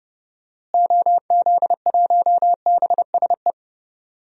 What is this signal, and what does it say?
Morse code, heard as a keyed tone, about 30 wpm, 700 Hz. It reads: OZ1BSE
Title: DXCC-100 Call Signs - 6 Characters - Repeated 3x - 30wpm